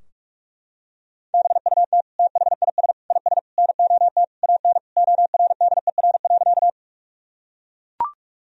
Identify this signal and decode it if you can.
Morse code, heard as a keyed tone, about 43 wpm, 700 Hz. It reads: BUT THIS IS NOT AN ORDER.
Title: Sentences from Top 1000 Words - Review Part 2 - 45wpm